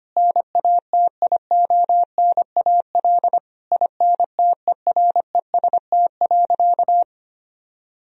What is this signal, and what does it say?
Morse code, heard as a keyed tone, about 25 wpm, 700 Hz. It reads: NATIONAL INTEREST.
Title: Sentences from Top 1000 Words - Review Part 4 - 25wpm